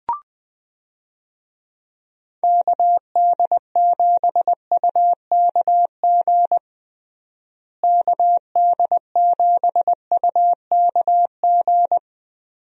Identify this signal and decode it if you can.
Morse code, heard as a keyed tone, about 21 wpm, 700 Hz. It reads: KD7UKG KD7UKG
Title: USA Call Signs - 6 Characters - Repeated 2x - 20wpm